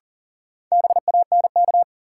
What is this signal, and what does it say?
Morse code, heard as a keyed tone, about 37 wpm, 700 Hz. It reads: BANK
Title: Sets of 5 Words - New Word Drill - 40wpm